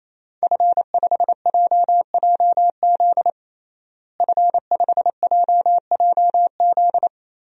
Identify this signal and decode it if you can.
Morse code, heard as a keyed tone, about 27 wpm, 700 Hz. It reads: F5JJZ F5JJZ
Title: DXCC-100 Call Signs - 5 Characters - Repeated 3x - 28wpm